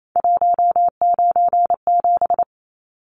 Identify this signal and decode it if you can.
Morse code, heard as a keyed tone, about 28 wpm, 700 Hz. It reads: 197
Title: Numbers - 3-Digit 28wpm